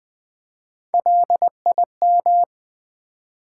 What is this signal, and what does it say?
Morse code, heard as a keyed tone, about 20 wpm, 700 Hz. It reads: LIM
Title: Build A Word - 1 Letter At A Time - 5-Letter Words - 20wpm